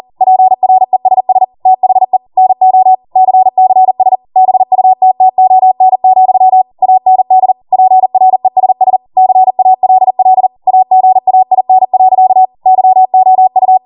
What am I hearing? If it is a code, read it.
Morse code, heard as a keyed tone, about 40 wpm, 750 Hz. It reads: PRESS THE NO CCS BUTTON, AND PRESS CALL AGAIN. YOU